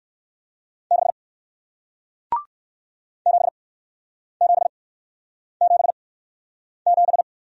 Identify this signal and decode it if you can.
Morse code, heard as a keyed tone, about 57 wpm, spaced as for 9 wpm, 700 Hz. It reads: 6 7777